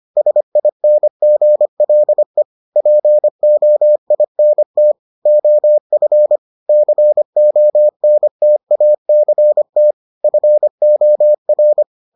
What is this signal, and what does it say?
Morse code, heard as a keyed tone, about 25 wpm, 600 Hz. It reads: SINGLE POINT OF CONTACT FOR